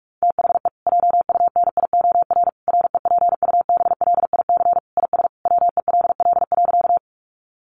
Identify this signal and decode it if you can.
Morse code, heard as a keyed tone, about 44 wpm, 700 Hz. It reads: THE JUNIOR REPUBLIC IS WELL.
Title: Sentences from Top 1500 Words 45wpm